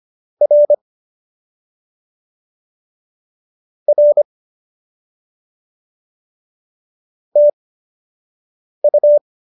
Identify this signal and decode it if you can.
Morse code, heard as a keyed tone, about 25 wpm, spaced as for 6 wpm, 570 Hz. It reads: R R TU